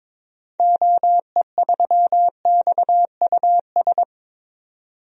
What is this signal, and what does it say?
Morse code, heard as a keyed tone, about 22 wpm, 700 Hz. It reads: OE3XUS
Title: DXCC-100 Call Signs - 6 Characters - 22wpm